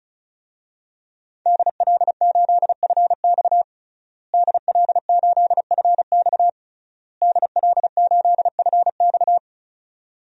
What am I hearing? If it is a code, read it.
Morse code, heard as a keyed tone, about 36 wpm, 700 Hz. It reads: DL8FX DL8FX DL8FX
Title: DXCC-100 Call Signs - 5 Characters - Repeated 3x - 35wpm